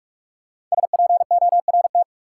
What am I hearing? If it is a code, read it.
Morse code, heard as a keyed tone, about 47 wpm, 700 Hz. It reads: SPORT
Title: Five Letter Words - 45wpm